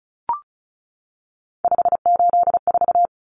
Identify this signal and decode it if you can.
Morse code, heard as a keyed tone, about 34 wpm, 700 Hz. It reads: 584
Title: Common Cut Numbers - 3-Digit 35wpm